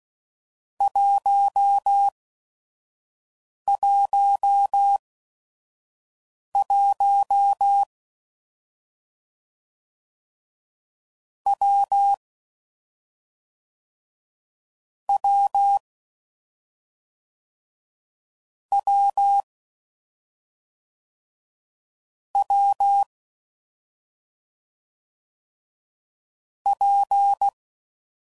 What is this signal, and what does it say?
Morse code, heard as a keyed tone, about 16 wpm, spaced as for 5 wpm, 800 Hz. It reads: 111 W W W W P
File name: Combinations / Difficult Combos / AWJ1